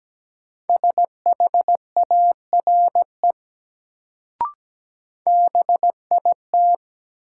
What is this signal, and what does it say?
Morse code, heard as a keyed tone, about 17 wpm, 700 Hz. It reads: SHARE BIT
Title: Sentences from Top 900 Words - New Word Drill - 17wpm